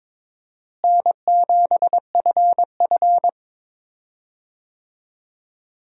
Morse code, heard as a keyed tone, about 22 wpm, 700 Hz. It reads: N7FF
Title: USA Call Signs - 4 Characters - Repeated 2x - 22wpm